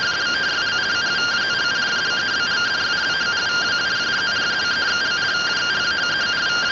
Two-tone teleprinter signal, about 170 Hz shift.